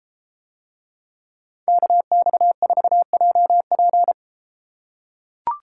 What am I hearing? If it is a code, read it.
Morse code, heard as a keyed tone, about 31 wpm, 700 Hz. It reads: KX4JP